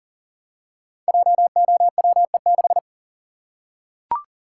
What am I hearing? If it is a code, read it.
Morse code, heard as a keyed tone, about 42 wpm, 700 Hz. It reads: JOWE6
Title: Code Groups - 5 Letter:Numbers - 40wpm